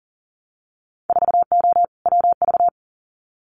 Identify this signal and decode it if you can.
Morse code, heard as a keyed tone, about 38 wpm, 700 Hz. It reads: 4O WV
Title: Winter Field Day Exchanges Encore 40wpm